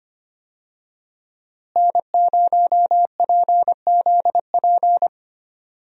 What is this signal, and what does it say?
Morse code, heard as a keyed tone, about 25 wpm, 700 Hz. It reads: N0PZP
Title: USA Call Signs - 5 Characters - Repeated 3x - 25wpm